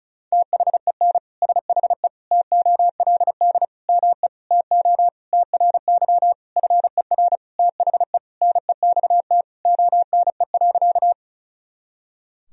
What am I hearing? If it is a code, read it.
Morse code, heard as a keyed tone, about 36 wpm, 700 Hz. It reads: THEN SHE TOLD ME TO TRY FER THE NEXT ONE.
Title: Sentences from Top 400 Words and Abbreviations 35wpm